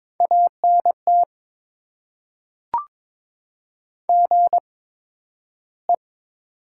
Morse code, heard as a keyed tone, about 22 wpm, spaced as for 12 wpm, 700 Hz. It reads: ANT GE
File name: Top 20 QSO Elements - 8x Character Spacing - 22wpm